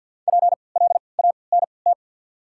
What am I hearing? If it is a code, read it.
Morse code, heard as a keyed tone, about 53 wpm, spaced as for 29 wpm, 690 Hz. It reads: PLANT
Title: 3x Character Spacing - Top 500 Words - 50wpm